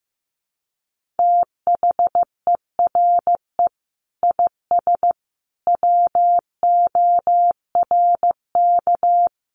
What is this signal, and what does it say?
Morse code, heard as a keyed tone, about 15 wpm, 700 Hz. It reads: THERE IS WORK
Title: Sentences from Top 1400 Words 15wpm